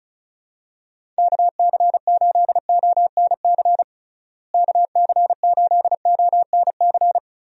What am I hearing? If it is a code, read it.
Morse code, heard as a keyed tone, about 36 wpm, 700 Hz. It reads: KC8ONC KC8ONC
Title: USA Call Signs - 6 Characters - Repeated 3x - 35wpm